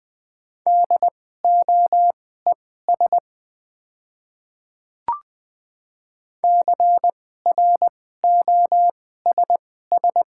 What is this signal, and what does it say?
Morse code, heard as a keyed tone, about 20 wpm, spaced as for 14 wpm, 700 Hz. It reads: DOES CROSS
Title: 2x Character Spacing - Top 500 Words - 20wpm